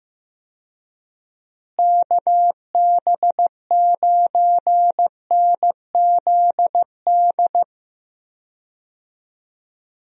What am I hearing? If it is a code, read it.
Morse code, heard as a keyed tone, about 15 wpm, 700 Hz. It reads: KB9NZD
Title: USA Call Signs - 6 Characters - Repeated 2x - 15wpm